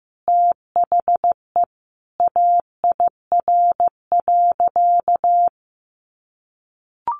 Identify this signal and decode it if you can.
Morse code, heard as a keyed tone, about 14 wpm, 700 Hz. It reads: THE AIR.
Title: Additional Easy Phrases 15wpm